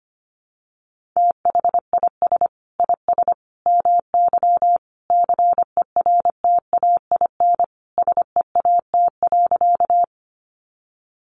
Morse code, heard as a keyed tone, about 25 wpm, 700 Hz. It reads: THIS IS MY CERTAIN SEAT.
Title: Sentences from Top 1500 Words 25wpm